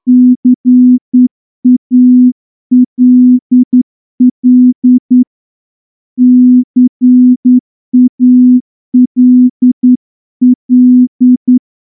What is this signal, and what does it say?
Morse code, heard as a keyed tone, about 10 wpm, 250 Hz. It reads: CALL CALL
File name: morse-training-10WPM